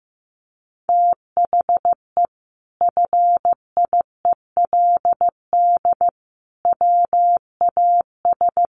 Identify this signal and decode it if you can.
Morse code, heard as a keyed tone, about 15 wpm, 700 Hz. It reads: THE FIELD WAS
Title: Sentences from Top 1300 Words 15wpm